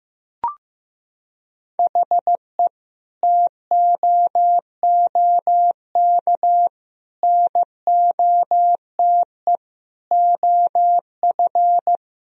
Morse code, heard as a keyed tone, about 16 wpm, 700 Hz. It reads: HE TOOK NOTE OF
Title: Sentences from Top 1000 Words - Review Part 2 - 15wpm